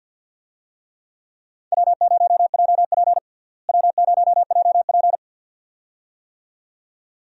Morse code, heard as a keyed tone, about 52 wpm, 700 Hz. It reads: W0JP W0JP
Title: USA Call Signs - 4 Characters - Repeated 2x - 50wpm